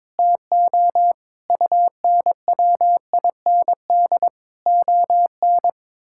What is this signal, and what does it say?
Morse code, heard as a keyed tone, about 22 wpm, 700 Hz. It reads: TO UNWIND ON
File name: Difficult Sentences with Hints - 22wpm